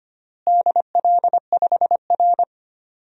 Morse code, heard as a keyed tone, about 25 wpm, 700 Hz. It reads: DL5R
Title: DXCC-100 Call Signs - 4 Characters - Repeated 3x - 25wpm